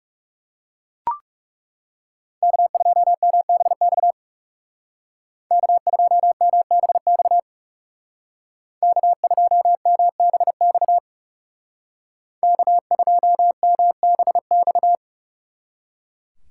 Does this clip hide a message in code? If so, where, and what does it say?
Yes: Morse code from 2.4 to 14.9 s, heard as a keyed tone, about 39 wpm, 700 Hz. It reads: K2MBX K2MBX K2MBX K2MBX